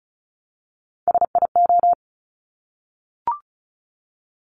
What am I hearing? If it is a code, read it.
Morse code, heard as a keyed tone, about 36 wpm, 700 Hz. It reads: SIO
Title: Top Three Letter Combinations in Words 35wpm